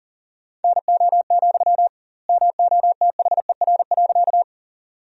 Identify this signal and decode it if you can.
Morse code, heard as a keyed tone, about 40 wpm, 700 Hz. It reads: NO, MOTHER.
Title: 2x Word-Spacing - Sentences from Top 200 Words - 40wpm